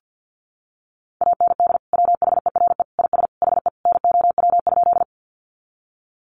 Morse code, heard as a keyed tone, about 50 wpm, 700 Hz. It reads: AND WHERE IS HE NOW?
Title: Sentences from Top 200 Words 50wpm